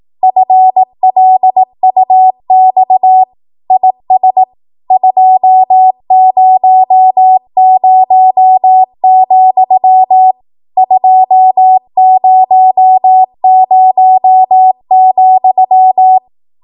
Morse code, heard as a keyed tone, about 18 wpm, 750 Hz. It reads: FLUX IS 200, 200,
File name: Morse / 18WPM